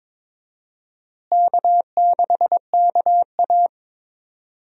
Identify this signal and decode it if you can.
Morse code, heard as a keyed tone, about 21 wpm, 700 Hz. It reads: K6KA
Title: USA Call Signs - 4 Characters - Repeated 2x - 22wpm